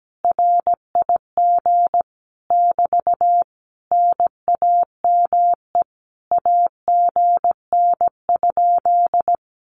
Morse code, heard as a keyed tone, about 17 wpm, 700 Hz. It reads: RIG <BT> NAME AGN?
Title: Common QSO Phrases Encore 17wpm